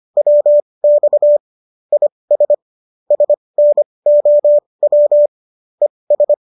Morse code, heard as a keyed tone, about 25 wpm, spaced as for 20 wpm, 590 Hz. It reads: WX IS SNOW ES